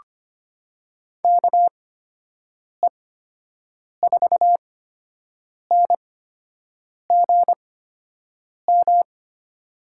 Morse code, heard as a keyed tone, about 25 wpm, spaced as for 7 wpm, 700 Hz. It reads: KE4NGM
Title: USA Call Signs - 6 Characters - 8x Character Spacing - 25wpm